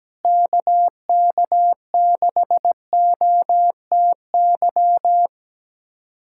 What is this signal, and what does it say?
Morse code, heard as a keyed tone, about 17 wpm, 700 Hz. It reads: KK6OTY
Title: USA Callsign DE Callsign - 6 Characters - 17wpm